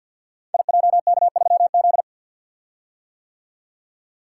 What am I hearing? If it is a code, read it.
Morse code, heard as a keyed tone, about 48 wpm, 700 Hz. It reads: IJK3Z
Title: Code Groups - 5 Letter:Numbers - 50wpm